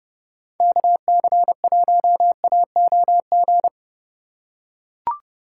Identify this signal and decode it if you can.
Morse code, heard as a keyed tone, about 29 wpm, 700 Hz. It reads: KC1AOG